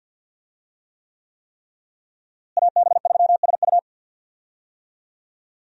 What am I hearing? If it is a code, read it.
Morse code, heard as a keyed tone, about 46 wpm, 700 Hz. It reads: AB3SU
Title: USA Call Signs - 5 Characters - Repeated 3x - 50wpm